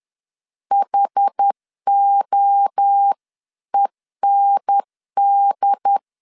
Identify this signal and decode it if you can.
Morse code, heard as a keyed tone, about 10 wpm, 800 Hz. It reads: HOEND
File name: 10wpm